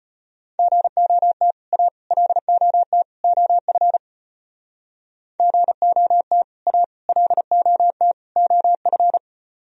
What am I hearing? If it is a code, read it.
Morse code, heard as a keyed tone, about 33 wpm, 700 Hz. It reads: GOT A LOT OF GOT A LOT OF